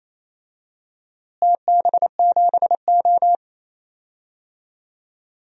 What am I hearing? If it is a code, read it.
Morse code, heard as a keyed tone, about 28 wpm, 700 Hz. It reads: TB7O
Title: DXCC-100 Call Signs - 4 Characters - Repeated 3x - 28wpm